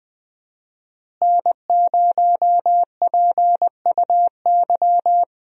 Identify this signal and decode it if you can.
Morse code, heard as a keyed tone, about 20 wpm, 700 Hz. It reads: N0PUY
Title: USA Call Signs - 5 Characters - Repeated 2x - 20wpm